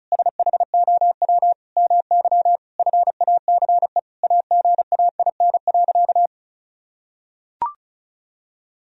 Morse code, heard as a keyed tone, about 36 wpm, 700 Hz. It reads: SHOW MY FACE AGAIN.
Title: Sentences from Top 1000 Words - Review Part 2 - 35wpm